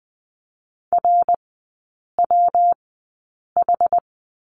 Morse code, heard as a keyed tone, about 20 wpm, spaced as for 8 wpm, 700 Hz. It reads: RWH
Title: 184 Lesson 30e - Letter b - Three Characters 20wpm